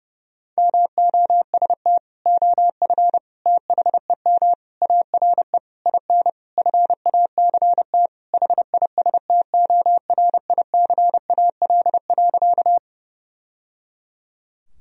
Morse code, heard as a keyed tone, about 30 wpm, 700 Hz. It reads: MOST OF THEM ARE IN FACT HISTORICAL.
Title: Sentences from Top 1000 Words - Review Part 4 - 30wpm